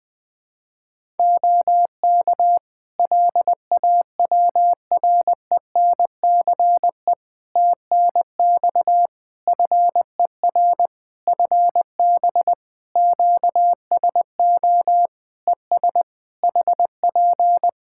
Morse code, heard as a keyed tone, about 20 wpm, 700 Hz. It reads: OK LAWRENCE TNX FER FB QSO ES HP
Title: Practice QSO - Exchange 4 - Ending - 20wpm